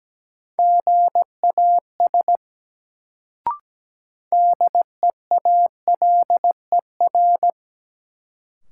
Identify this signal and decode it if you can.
Morse code, heard as a keyed tone, about 17 wpm, 700 Hz. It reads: GAS DEALER
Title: Sentences from Top 1100 Words - New Word Drill Alt - 17wpm